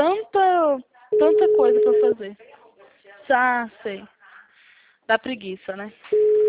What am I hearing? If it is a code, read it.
Ringback tone (Europe, 425 Hz), with speech.